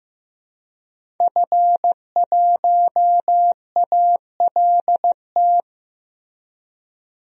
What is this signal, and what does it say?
Morse code, heard as a keyed tone, about 15 wpm, 700 Hz. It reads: F1ALT
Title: DXCC-100 Call Signs - 5 Characters - Repeated 3x - 15wpm